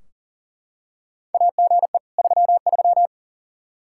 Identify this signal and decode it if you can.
Morse code, heard as a keyed tone, about 40 wpm, 700 Hz. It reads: AGE 33
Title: Practice QSO - Exchange 3 - More Info - 40wpm